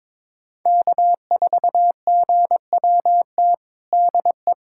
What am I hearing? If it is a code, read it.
Morse code, heard as a keyed tone, about 22 wpm, 700 Hz. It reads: K4GWT DE
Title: USA Callsign DE Callsign 22wpm